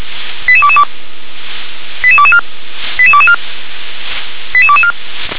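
Five-tone selective call (ZVEI), heard as quick runs of five tones, tones of about 70 ms. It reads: 88222 88224 88224 88224
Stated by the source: ZVEI five-tone selective call received on an SDR.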